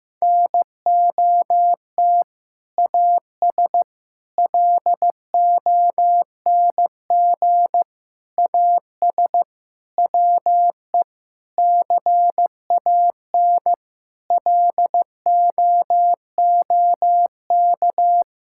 Morse code, heard as a keyed tone, about 15 wpm, 700 Hz. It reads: NOT AS LONG AS WE CAN LOOK
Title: Sentences from Top 1000 Words - Review Part 6 - 15wpm